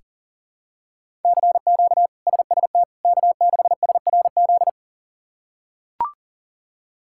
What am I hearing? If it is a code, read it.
Morse code, heard as a keyed tone, about 40 wpm, 700 Hz. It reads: CQ SST K6SRZ
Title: K1USN SST QSO Practice - 40wpm